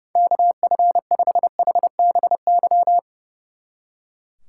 Morse code, heard as a keyed tone, about 34 wpm, 700 Hz. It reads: KF5HBY
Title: Speed Racing - USA Call Signs 30wpm